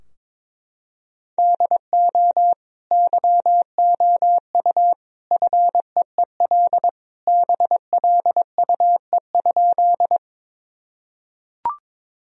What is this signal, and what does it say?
Morse code, heard as a keyed tone, about 22 wpm, 700 Hz. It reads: DO YOU FEEL BLUE?
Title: Sentences from Top 1000 Words - Review Part 3 - 22wpm